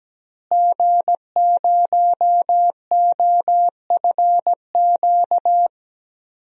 Morse code, heard as a keyed tone, about 17 wpm, 700 Hz. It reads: G0OFQ
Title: DXCC-100 Call Signs - 5 Characters - Repeated 2x - 17wpm